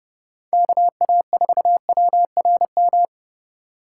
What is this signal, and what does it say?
Morse code, heard as a keyed tone, about 30 wpm, 700 Hz. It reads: KA4WRM